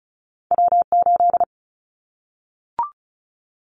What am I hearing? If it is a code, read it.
Morse code, heard as a keyed tone, about 34 wpm, 700 Hz. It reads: W8
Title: Callsign Prefixes for Top 25 DXCC - Repeated 3x - 35wpm